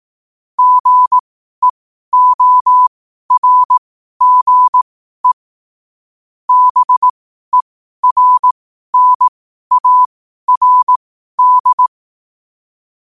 Morse code, heard as a keyed tone, about 18 wpm, spaced as for 12 wpm, 1000 Hz. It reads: GEORGE BERNARD